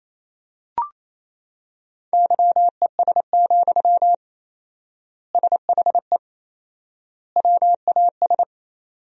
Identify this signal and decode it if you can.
Morse code, heard as a keyed tone, about 27 wpm, 700 Hz. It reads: YES, SHE WAS
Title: 4x Word-Spacing - Sentences from Top 500 Words - 28wpm